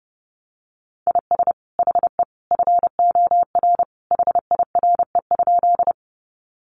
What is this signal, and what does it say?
Morse code, heard as a keyed tone, about 28 wpm, 700 Hz. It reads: IS HE FOR HIRE?
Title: Sentences from Top 1400 Words 30wpm